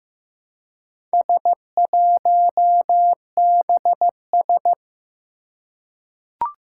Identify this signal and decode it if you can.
Morse code, heard as a keyed tone, about 15 wpm, 700 Hz. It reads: S1BS
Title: Four Challenging Letter-Number Combinations 15wpm